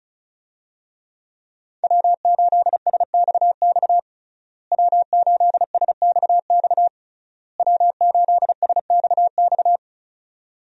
Morse code, heard as a keyed tone, about 36 wpm, 700 Hz. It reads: W8SXX W8SXX W8SXX
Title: USA Call Signs - 5 Characters - Repeated 3x - 35wpm